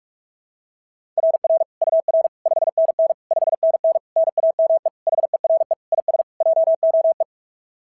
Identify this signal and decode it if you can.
Morse code, heard as a keyed tone, about 43 wpm, 660 Hz. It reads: RR UR 5NN 5NN NAME HERE IS JOE
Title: SOTA QSO Practice 45wpm